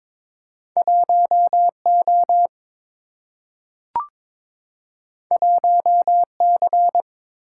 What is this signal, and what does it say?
Morse code, heard as a keyed tone, about 22 wpm, 700 Hz. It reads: 1 O 1 C
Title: Single-Letter Single-Number 22wpm